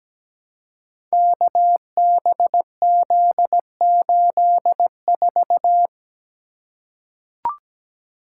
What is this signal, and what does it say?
Morse code, heard as a keyed tone, about 17 wpm, 700 Hz. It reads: KBZ84
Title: Code Groups - 5 Letter:Numbers - 17wpm